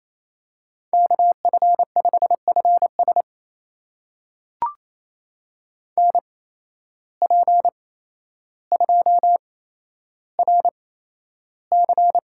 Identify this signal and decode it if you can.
Morse code, heard as a keyed tone, about 28 wpm, spaced as for 12 wpm, 700 Hz. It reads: KF5FS NP2RC
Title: USA Call Signs - 8x Character Spacing - 28wpm